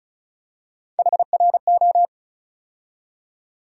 Morse code, heard as a keyed tone, about 36 wpm, 700 Hz. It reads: HRO
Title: USA Call Sign Postfixes - 3 Letters 35wpm